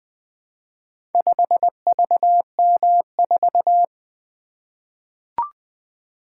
Morse code, heard as a keyed tone, about 21 wpm, 700 Hz. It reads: 5VM4
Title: Four Letter-Number Combinations - 20wpm